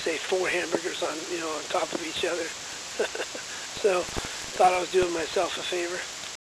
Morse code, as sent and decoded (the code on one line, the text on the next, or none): none